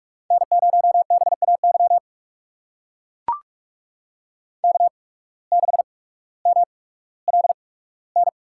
Morse code, heard as a keyed tone, about 44 wpm, spaced as for 19 wpm, 700 Hz. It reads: N0BAY K6MLN
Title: USA Call Signs - 5 Characters - 8x Character Spacing - 45wpm